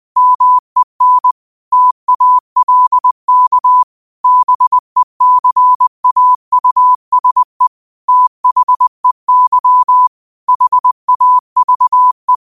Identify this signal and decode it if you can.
Morse code, heard as a keyed tone, about 20 wpm, 1000 Hz. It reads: MEN TALK BECAUSE THEY HAVE
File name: qotdfiles / 20wpm